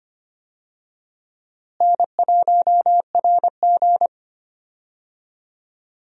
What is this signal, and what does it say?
Morse code, heard as a keyed tone, about 25 wpm, 700 Hz. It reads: N1RG
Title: USA Call Signs - 4 Characters - Repeated 2x - 25wpm